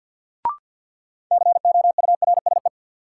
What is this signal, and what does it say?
Morse code, heard as a keyed tone, about 47 wpm, 700 Hz. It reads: COURSE
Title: Sets of 2 Words - New Word Drill - 50wpm